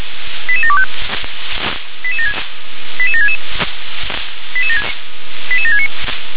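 Five-tone selective call (ZVEI), heard as quick runs of five tones, tones of about 70 ms. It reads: 88725 88766 88769
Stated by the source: ZVEI five-tone call 88757 received on an SDR.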